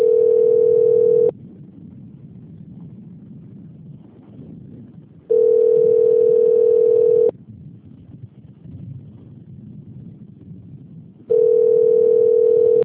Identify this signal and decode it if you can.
Ringback tone (North America), 440 + 480 Hz.